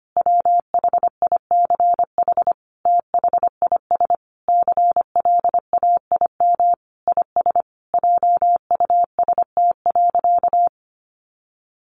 Morse code, heard as a keyed tone, about 25 wpm, 700 Hz. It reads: WHICH THIS CLAIM IS JUST.
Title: Sentences from Top 1500 Words 25wpm